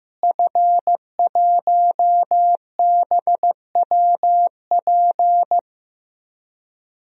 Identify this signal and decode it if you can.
Morse code, heard as a keyed tone, about 15 wpm, 700 Hz. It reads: F1BWP
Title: DXCC-100 Call Signs - 5 Characters - Repeated 2x - 15wpm